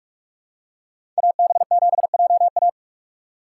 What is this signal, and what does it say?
Morse code, heard as a keyed tone, about 44 wpm, 700 Hz. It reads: AB7JA
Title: USA Call Signs - 5 Characters - 6x Character Spacing - 45wpm